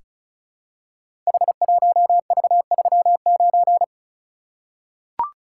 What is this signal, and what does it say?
Morse code, heard as a keyed tone, about 36 wpm, 700 Hz. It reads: H1V39
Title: Code Groups - 5 Letter:Numbers - 35wpm